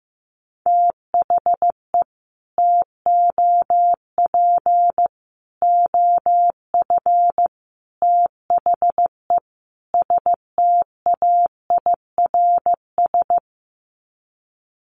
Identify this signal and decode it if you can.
Morse code, heard as a keyed tone, about 15 wpm, 700 Hz. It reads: THE TOP OF THE STAIRS
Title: Sets of 5 Words Encore 15wpm